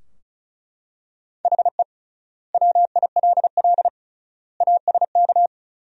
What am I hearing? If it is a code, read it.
Morse code, heard as a keyed tone, about 36 wpm, 700 Hz. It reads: HE WILL ASK
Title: 3x Word-Spacing - Sentences from Top 200 Words - 35wpm